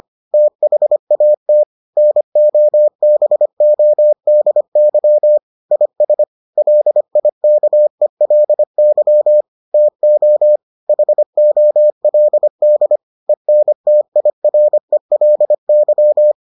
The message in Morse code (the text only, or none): THAT NOBODY IS LIKELY TO HOLD ENTIRELY